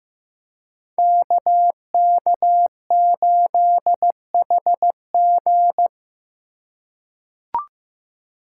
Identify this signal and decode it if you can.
Morse code, heard as a keyed tone, about 15 wpm, 700 Hz. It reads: KK8HG
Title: Code Groups - 5 Letter:Numbers - 15wpm